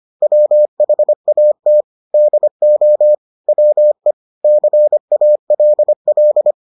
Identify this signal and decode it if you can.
Morse code, heard as a keyed tone, about 25 wpm, 600 Hz. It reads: WHAT DO WE CALL